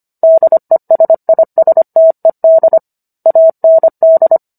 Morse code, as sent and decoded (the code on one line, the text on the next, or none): -.. ... .. ... - . -.. / .- -. -..
DESISTED AND